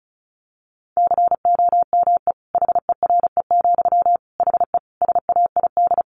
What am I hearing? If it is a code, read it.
Morse code, heard as a keyed tone, about 34 wpm, 700 Hz. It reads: COME HERE, HE SAID
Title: Sentences from Top 1500 Words 35wpm